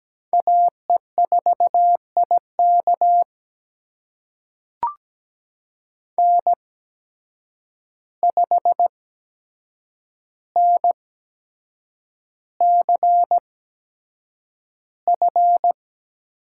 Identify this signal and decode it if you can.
Morse code, heard as a keyed tone, about 17 wpm, spaced as for 7 wpm, 700 Hz. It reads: AE4IK N5NCF
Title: USA Call Signs - 5 Characters - 8x Character Spacing - 17wpm